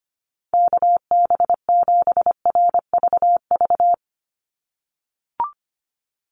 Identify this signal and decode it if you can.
Morse code, heard as a keyed tone, about 25 wpm, 700 Hz. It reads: KB7RVV
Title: USA Call Signs 25wpm